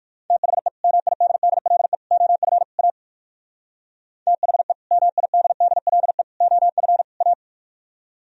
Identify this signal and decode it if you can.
Morse code, heard as a keyed tone, about 61 wpm, 700 Hz. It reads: THE MIDDLE OF A THE MIDDLE OF A